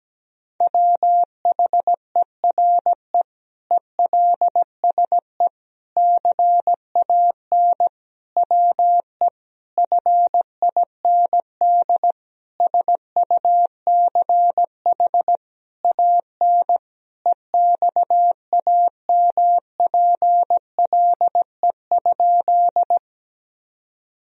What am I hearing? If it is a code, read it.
Morse code, heard as a keyed tone, about 17 wpm, 700 Hz. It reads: WHERE ELSE CAN WE FIND SUCH AN EXAMPLE?
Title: Sentences from Top 1000 Words - Review Part 1 - 17wpm